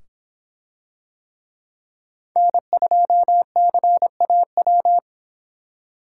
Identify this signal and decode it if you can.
Morse code, heard as a keyed tone, about 26 wpm, 700 Hz. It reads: N2CAW